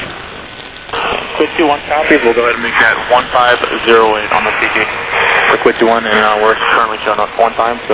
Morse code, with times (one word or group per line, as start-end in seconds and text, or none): none